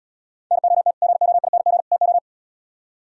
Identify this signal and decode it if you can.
Morse code, heard as a keyed tone, about 77 wpm, 700 Hz. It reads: NOT GOING TO